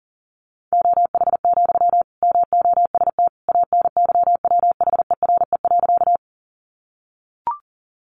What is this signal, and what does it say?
Morse code, heard as a keyed tone, about 40 wpm, 700 Hz. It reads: OH, MOST ANYWHERE.
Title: Sentences from Top 1400 Words 40wpm